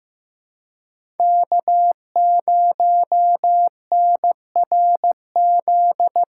Morse code, heard as a keyed tone, about 18 wpm, 700 Hz. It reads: K0NRZ
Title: Speed Racing - USA 1x3 Call Signs 15wpm